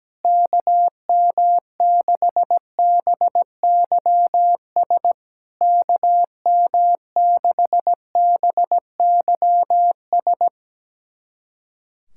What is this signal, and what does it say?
Morse code, heard as a keyed tone, about 17 wpm, 690 Hz. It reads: KM6BYS KM6BYS
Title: SOTA QSO Practice 17wpm